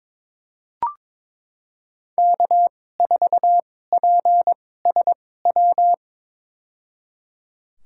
Morse code, heard as a keyed tone, about 22 wpm, spaced as for 16 wpm, 700 Hz. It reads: K4PSW
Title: USA Call Signs - 2x Character Spacing - 22wpm